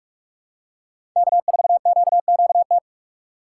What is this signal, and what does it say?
Morse code, heard as a keyed tone, about 45 wpm, 700 Hz. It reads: K4QQT
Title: USA Call Signs - 5 Characters - 4x Character Spacing - 45wpm